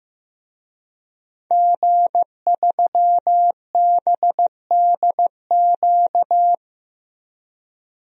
Morse code, heard as a keyed tone, about 15 wpm, 700 Hz. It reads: G3BDQ
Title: DXCC-100 Call Signs - 5 Characters - 15wpm